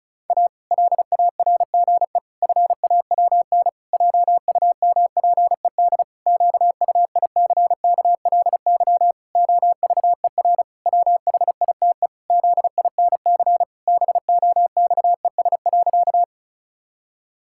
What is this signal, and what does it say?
Morse code, heard as a keyed tone, about 34 wpm, 700 Hz. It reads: A LARGE FAWN JUMPED QUICKLY OVER WHITE ZINC BOXES.
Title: Pangram Challenge v2 - 35wpm